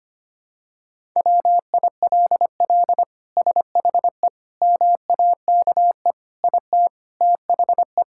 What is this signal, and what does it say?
Morse code, heard as a keyed tone, about 25 wpm, 700 Hz. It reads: WILL SHE MAKE IT THE
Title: Sentences from Top 1000 Words - Review Part 5 - 25wpm